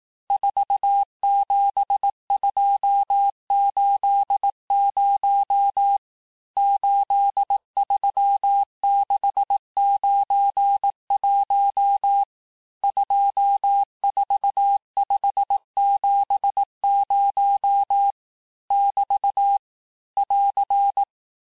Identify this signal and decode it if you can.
Morse code, heard as a keyed tone, about 18 wpm, 800 Hz. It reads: 47280 83691 24570 <BT> <AR>
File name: cours-du-lundi files / cours-1594--teste-chiffre-vitesse-17mpm